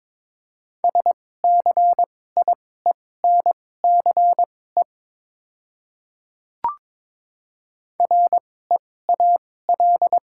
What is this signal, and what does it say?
Morse code, heard as a keyed tone, about 22 wpm, spaced as for 16 wpm, 700 Hz. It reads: SCIENCE REAL
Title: 2x Character Spacing - Top 500 Words - 22wpm